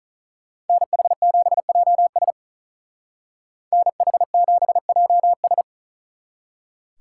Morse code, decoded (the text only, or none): NH7JS NH7JS